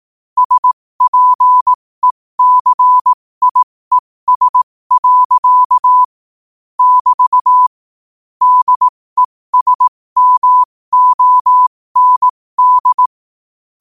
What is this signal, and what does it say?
Morse code, heard as a keyed tone, about 18 wpm, spaced as for 15 wpm, 1000 Hz. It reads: SPECIES. <BT> DESMOND
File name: qotdfiles / 15wpm